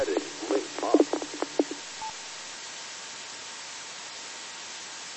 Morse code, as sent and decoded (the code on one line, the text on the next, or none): none